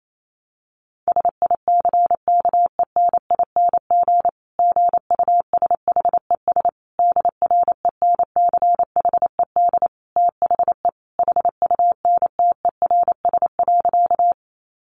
Morse code, heard as a keyed tone, about 28 wpm, 700 Hz. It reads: SICKENING GUSHES DRENCHED THE HUNTERS.
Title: A Sound Of Thunder 28wpm